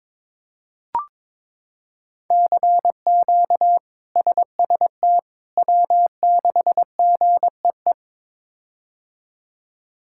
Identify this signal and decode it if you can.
Morse code, heard as a keyed tone, about 21 wpm, 700 Hz. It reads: CQ SST W6GEE
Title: K1USN SST QSO Practice - 22wpm